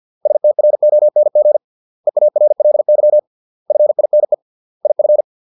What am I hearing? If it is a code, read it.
Morse code, heard as a keyed tone, about 50 wpm, 600 Hz. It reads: STRONG EARLY FINE IF
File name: 632 words / 632 Words 50WPM